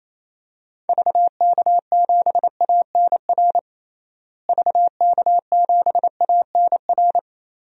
Morse code, heard as a keyed tone, about 28 wpm, 700 Hz. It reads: VK7ANR VK7ANR
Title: DXCC-100 Call Signs - 6 Characters - Repeated 2x - 28wpm